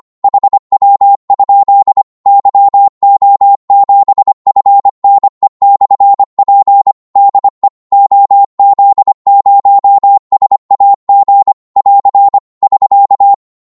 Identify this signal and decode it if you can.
Morse code, heard as a keyed tone, about 25 wpm, 800 Hz. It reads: HW? YO7FNE/P DE OZ0SAG <AR> <SK>
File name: Morse / qso-25wpm